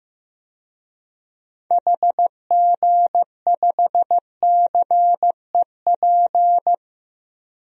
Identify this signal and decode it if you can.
Morse code, heard as a keyed tone, about 15 wpm, 700 Hz. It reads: HG5CEP
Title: DXCC-100 Call Signs - 6 Characters - 15wpm